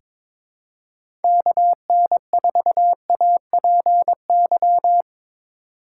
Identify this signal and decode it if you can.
Morse code, heard as a keyed tone, about 22 wpm, 700 Hz. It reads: KN4APY
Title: USA Call Signs - 6 Characters - 2x Character Spacing - 22wpm